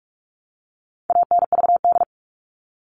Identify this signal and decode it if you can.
Morse code, heard as a keyed tone, about 44 wpm, 700 Hz. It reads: ANVD
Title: All Cut Numbers - 4-Digit 45wpm